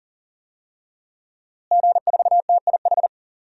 Morse code, heard as a keyed tone, about 40 wpm, 700 Hz. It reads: G4TIH
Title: DXCC-100 Call Signs - 5 Characters - Repeated 2x - 40wpm